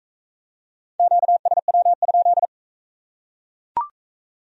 Morse code, heard as a keyed tone, about 40 wpm, 700 Hz. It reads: QSW?